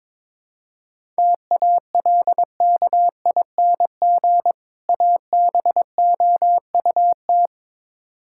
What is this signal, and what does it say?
Morse code, heard as a keyed tone, about 22 wpm, 700 Hz. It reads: TALKING ABOUT
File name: Build A Phrase - 2-Word Phrases - 22wpm